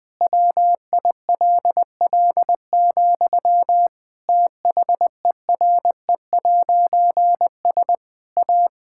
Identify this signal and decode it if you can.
Morse code, heard as a keyed tone, about 21 wpm, 700 Hz. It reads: WILL, THERE'S A
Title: Difficult Sentences with Hints - 20wpm